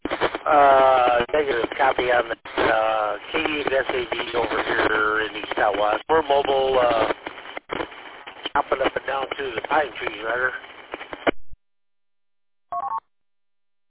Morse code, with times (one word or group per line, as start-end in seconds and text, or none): none